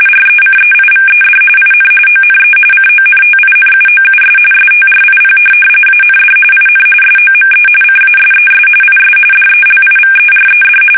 RTTY, about 45.45 baud. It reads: RECEIVED IN ONE CONTINUOUS STRIP AND IN SOME CASES IS DELIVERED
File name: fsk850